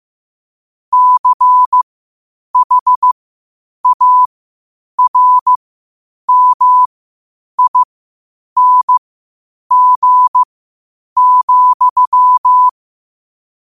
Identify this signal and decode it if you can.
Morse code, heard as a keyed tone, about 15 wpm, spaced as for 8 wpm, 1000 Hz. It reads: CHARMING,
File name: qotdfiles / 8wpm